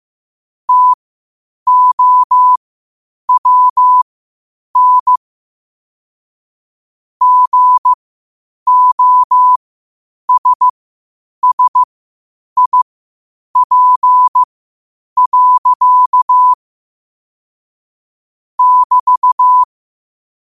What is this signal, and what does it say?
Morse code, heard as a keyed tone, about 15 wpm, spaced as for 8 wpm, 1000 Hz. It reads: TOWN GOSSIP. <BT>
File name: qotdfiles / 8wpm